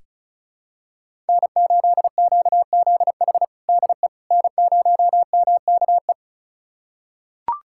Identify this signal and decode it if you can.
Morse code, heard as a keyed tone, about 34 wpm, 700 Hz. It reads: N8QZH DE N0MKE
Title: USA Callsign DE Callsign - 5 Characters - 35wpm